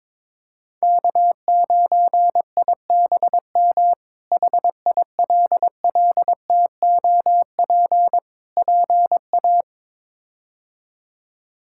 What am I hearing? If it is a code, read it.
Morse code, heard as a keyed tone, about 22 wpm, 700 Hz. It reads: K9IBM HILLTOP PA
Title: K1USN SST QSO Practice - 22wpm